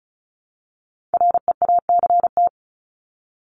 Morse code, heard as a keyed tone, about 34 wpm, 700 Hz. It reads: REACT
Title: Top 1251-1500 Words 35wpm